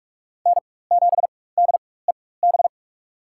Morse code, heard as a keyed tone, about 47 wpm, spaced as for 21 wpm, 700 Hz. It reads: N7DEB
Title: USA Call Signs - 5 Characters - 4x Character Spacing - 45wpm